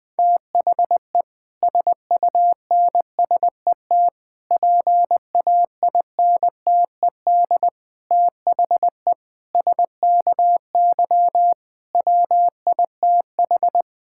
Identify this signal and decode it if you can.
Morse code, heard as a keyed tone, about 20 wpm, 700 Hz. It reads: THE SUNSET PAINTED THE SKY WITH
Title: Difficult Sentences with Hints - 20wpm